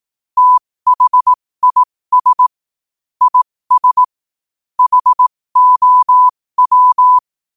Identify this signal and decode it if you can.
Morse code, heard as a keyed tone, about 18 wpm, spaced as for 15 wpm, 1000 Hz. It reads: THIS IS HOW
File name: qotdfiles / 15wpm